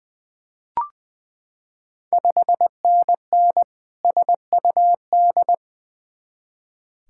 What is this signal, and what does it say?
Morse code, heard as a keyed tone, about 20 wpm, 700 Hz. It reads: 5NN SUD
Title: Ontario QSO Party Exchange 20wpm